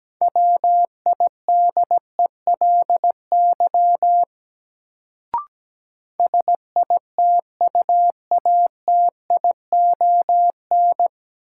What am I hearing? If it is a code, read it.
Morse code, heard as a keyed tone, about 17 wpm, 700 Hz. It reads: WIDELY SITUATION
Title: Sentences from Top 1200 Words - New Word Drill Alt - 17wpm